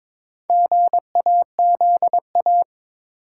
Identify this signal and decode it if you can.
Morse code, heard as a keyed tone, about 21 wpm, 700 Hz. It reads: GAZA
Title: Sets of 5 Words - New Word Drill - 22wpm